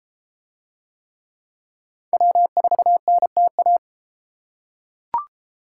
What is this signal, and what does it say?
Morse code, heard as a keyed tone, about 33 wpm, 700 Hz. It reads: W4NTA